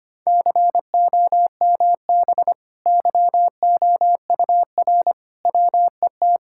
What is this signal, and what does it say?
Morse code, heard as a keyed tone, about 25 wpm, 700 Hz. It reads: COMB YOUR WET
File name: Easy Sentences with Hints - 25wpm